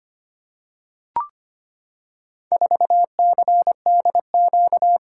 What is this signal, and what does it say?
Morse code, heard as a keyed tone, about 25 wpm, 700 Hz. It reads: 4CDQ
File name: practice / 25wpm / Four Letter-Number Combinations - 25wpm